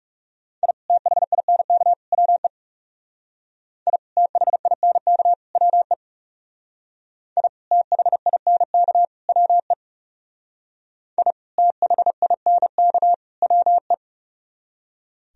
Morse code, heard as a keyed tone, about 36 wpm, 700 Hz. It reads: I THINK WE I THINK WE I THINK WE I THINK WE
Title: Speed-Racing - Sets of 3 Words 30wpm